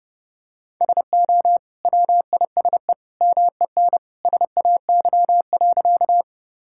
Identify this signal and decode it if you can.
Morse code, heard as a keyed tone, about 30 wpm, 700 Hz. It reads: SO WISE MEN SAY.
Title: Sentences from Top 1600 Words 30wpm